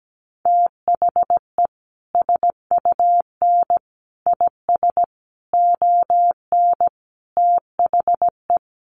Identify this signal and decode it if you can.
Morse code, heard as a keyed tone, about 17 wpm, 700 Hz. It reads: THE SUN IS ON THE
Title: Sentences from Top 1300 Words 17wpm